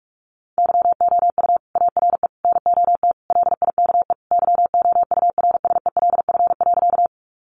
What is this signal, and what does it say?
Morse code, heard as a keyed tone, about 44 wpm, 700 Hz. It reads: YOU ARE NOT LIKE YOURSELF.
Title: Sentences from Top 600 Words 45wpm